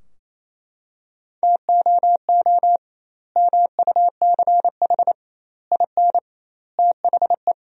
Morse code, heard as a keyed tone, about 28 wpm, 700 Hz. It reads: TOO MUCH IN THE
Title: 2x Word-Spacing - Sentences from Top 200 Words - 28wpm